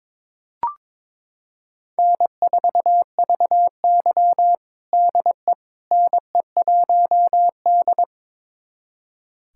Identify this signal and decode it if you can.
Morse code, heard as a keyed tone, about 22 wpm, 700 Hz. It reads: N4VY DE NE1D
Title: USA Callsign DE Callsign - 4 Characters - 22wpm